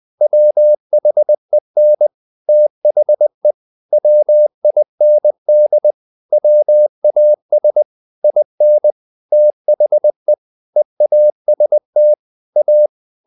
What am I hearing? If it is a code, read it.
Morse code, heard as a keyed tone, about 20 wpm, 600 Hz. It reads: WHEN THE WIND WAS IN THE EAST A
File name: the old man and the sea / the old man and the sea part 1 20wpm